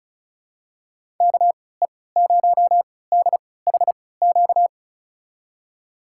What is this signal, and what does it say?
Morse code, heard as a keyed tone, about 36 wpm, spaced as for 20 wpm, 700 Hz. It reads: KE0DHQ
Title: USA Call Signs - 6 Characters - 3x Character Spacing - 35wpm